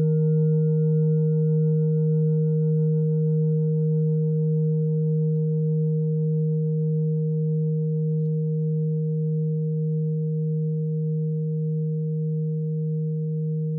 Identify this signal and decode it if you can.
There is no code.